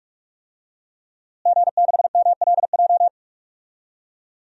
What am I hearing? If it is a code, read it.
Morse code, heard as a keyed tone, about 46 wpm, 700 Hz. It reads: G6MLJ
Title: DXCC-100 Call Signs - 5 Characters - Repeated 2x - 45wpm